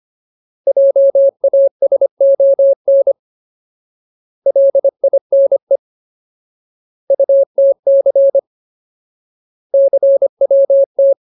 Morse code, heard as a keyed tone, about 25 wpm, 550 Hz. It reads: JASON LINE UTC CWT
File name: Ultimate Word List / 25WPM Ultimate